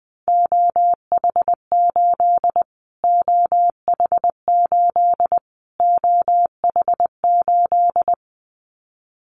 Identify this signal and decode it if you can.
Morse code, heard as a keyed tone, about 20 wpm, 700 Hz. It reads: OH8 OH8 OH8
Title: Callsign Prefixes for Top 25 DXCC - Repeated 3x - 20wpm